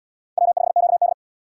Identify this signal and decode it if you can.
Morse code, heard as a keyed tone, about 67 wpm, 700 Hz. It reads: W5QD